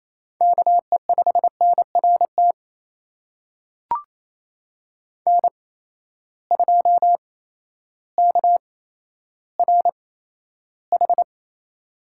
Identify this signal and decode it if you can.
Morse code, heard as a keyed tone, about 28 wpm, spaced as for 13 wpm, 700 Hz. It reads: KE5NRT N2KRH
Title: USA Call Signs - 8x Character Spacing - 28wpm